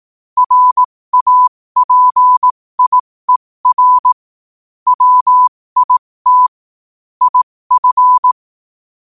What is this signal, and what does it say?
Morse code, heard as a keyed tone, about 18 wpm, spaced as for 15 wpm, 1000 Hz. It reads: RAPIER WIT IF